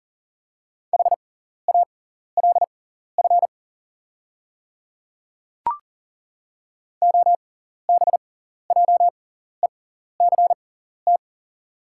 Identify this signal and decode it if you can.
Morse code, heard as a keyed tone, about 38 wpm, spaced as for 14 wpm, 690 Hz. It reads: HALF OBJECT
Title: 6x Character Spacing - Top 500 Words - 40wpm